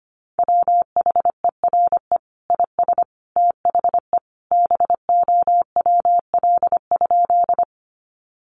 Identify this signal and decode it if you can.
Morse code, heard as a keyed tone, about 25 wpm, 700 Hz. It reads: WHERE IS THE BOWL?
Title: Sentences from Top 1200 Words 25wpm